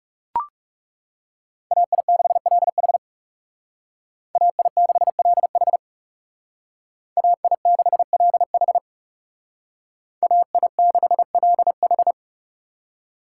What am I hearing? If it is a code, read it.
Morse code, heard as a keyed tone, about 35 wpm, 700 Hz. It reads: AI6LH AI6LH AI6LH AI6LH
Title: Speed Racing - USA 2x2 Call Signs 30wpm